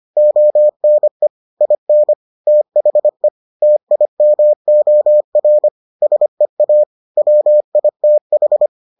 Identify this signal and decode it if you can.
Morse code, heard as a keyed tone, about 25 wpm, 600 Hz. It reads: ONE IN THE TIMOR SEA WITH